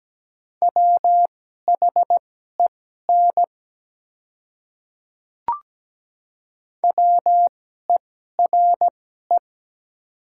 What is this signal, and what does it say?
Morse code, heard as a keyed tone, about 17 wpm, spaced as for 12 wpm, 700 Hz. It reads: WHEN WERE
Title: 2x Character Spacing - Top 200 Words - 17wpm